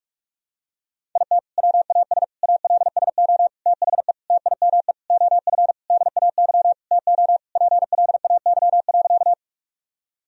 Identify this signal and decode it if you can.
Morse code, heard as a keyed tone, about 48 wpm, 700 Hz. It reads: IT WAS ALSO THE TIME OF DAY TO PLAY.
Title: Sentences from Top 1000 Words - Review Part 1 - 45wpm